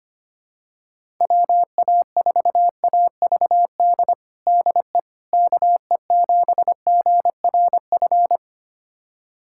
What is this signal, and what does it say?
Morse code, heard as a keyed tone, about 25 wpm, 700 Hz. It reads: WA4AVD DE KE7GRF
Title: USA Callsign DE Callsign 25wpm